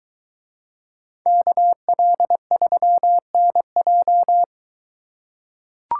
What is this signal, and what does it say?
Morse code, heard as a keyed tone, about 23 wpm, 700 Hz. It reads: KL3NJ